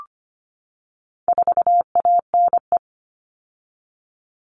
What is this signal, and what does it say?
Morse code, heard as a keyed tone, about 25 wpm, 700 Hz. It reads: 4ANE
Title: All Cut Numbers - 4-Digit 25wpm